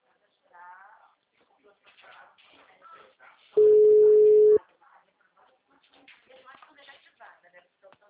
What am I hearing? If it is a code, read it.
Ringback tone (Europe, 425 Hz), with speech.
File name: out- -2010